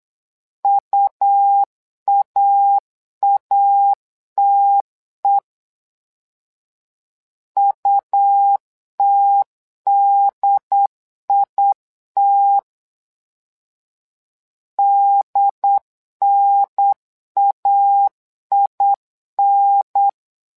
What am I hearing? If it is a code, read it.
Morse code, heard as a keyed tone, about 8 wpm, 800 Hz. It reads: UAATE UTDIT DNAIN
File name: LEZIONI / 2 UHDN / 2 UHDN 8wpm2232